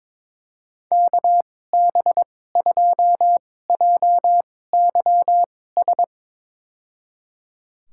Morse code, heard as a keyed tone, about 22 wpm, spaced as for 16 wpm, 700 Hz. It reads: KB2JYS
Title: USA Call Signs - 6 Characters - 2x Character Spacing - 22wpm